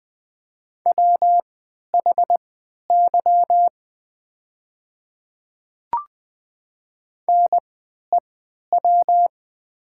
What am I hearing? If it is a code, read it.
Morse code, heard as a keyed tone, about 20 wpm, spaced as for 11 wpm, 700 Hz. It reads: WHY NEW
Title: 3x Character Spacing - Top 300 Words - 20wpm